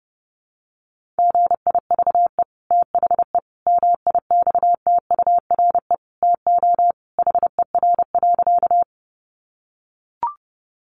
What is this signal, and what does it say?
Morse code, heard as a keyed tone, about 30 wpm, 700 Hz. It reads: GIVE THE MIXTURE TO HER.
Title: Sentences from Top 1500 Words 30wpm